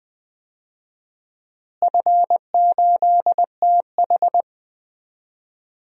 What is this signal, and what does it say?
Morse code, heard as a keyed tone, about 20 wpm, 700 Hz. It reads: F8TH
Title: DXCC-100 Call Signs - 4 Characters - Repeated 2x - 20wpm